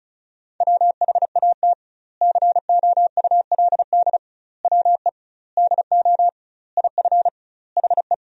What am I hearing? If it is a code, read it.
Morse code, heard as a keyed tone, about 36 wpm, 700 Hz. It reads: WHAT COULD WE DO IF HE
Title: 2x Word-Spacing - Sentences from Top 400 Words - 35wpm